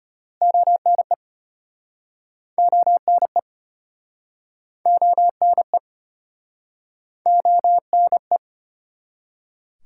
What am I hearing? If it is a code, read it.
Morse code, heard as a keyed tone, about 32 wpm, 700 Hz. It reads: ONE ONE ONE ONE